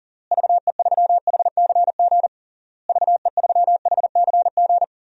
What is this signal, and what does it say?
Morse code, heard as a keyed tone, about 40 wpm, 700 Hz. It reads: VE3HCG VE3HCG
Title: DXCC-100 Call Signs - 6 Characters - Repeated 3x - 40wpm